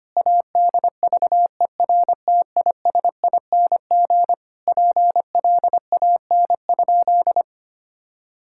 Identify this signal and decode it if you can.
Morse code, heard as a keyed tone, about 25 wpm, 700 Hz. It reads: ADVERTISING PLAN?
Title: Sentences from Top 1000 Words - Review Part 6 - 25wpm